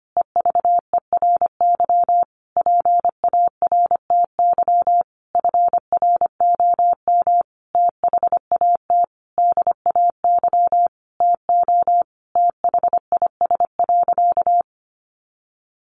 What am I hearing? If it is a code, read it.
Morse code, heard as a keyed tone, about 25 wpm, 700 Hz. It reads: EVERY PARTY FROM THAT DAY TO THIS.
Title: Sentences from Top 1400 Words 25wpm